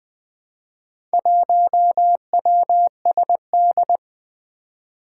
Morse code, heard as a keyed tone, about 20 wpm, 700 Hz. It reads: 1WSD
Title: Four Challenging Letter-Number Combinations 20wpm